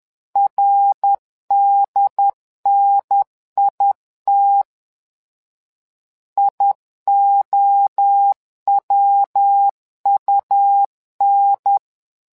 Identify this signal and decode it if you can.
Morse code, heard as a keyed tone, about 10 wpm, 800 Hz. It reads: RDNIT IOWUN
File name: LEZIONI / 3 MWRO / 3 MWRO 10wpm 2148